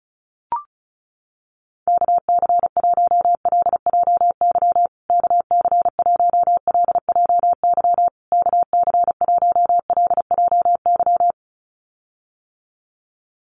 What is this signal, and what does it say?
Morse code, heard as a keyed tone, about 34 wpm, 700 Hz. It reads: KC1LJY KC1LJY KC1LJY
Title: USA Call Signs Repeated 3x 35wpm